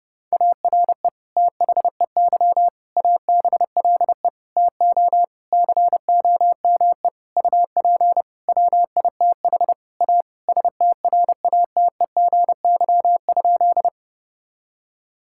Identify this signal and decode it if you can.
Morse code, heard as a keyed tone, about 31 wpm, 700 Hz. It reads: ARE THEY ABLE TO COME UP WITH A STRATEGY?
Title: Sentences from Top 1000 Words - Review Part 5 - 30wpm